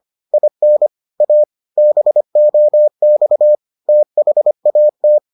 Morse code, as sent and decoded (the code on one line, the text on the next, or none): .. -. / .- / -... --- -..- / - .... .- -
IN A BOX THAT